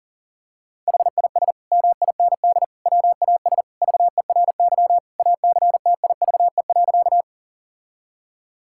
Morse code, heard as a keyed tone, about 40 wpm, 700 Hz. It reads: HIS MIND WAS VERY ACTIVE.
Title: Sentences from Top 1000 Words - Review Part 6 - 40wpm